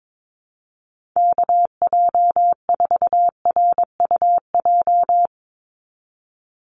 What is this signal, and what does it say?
Morse code, heard as a keyed tone, about 22 wpm, 700 Hz. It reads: KJ4RUJ
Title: USA Call Signs 22wpm